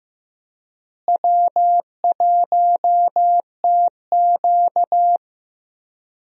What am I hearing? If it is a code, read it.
Morse code, heard as a keyed tone, about 15 wpm, 700 Hz. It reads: W1TQ
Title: USA Call Signs - 4 Characters - Repeated 2x - 15wpm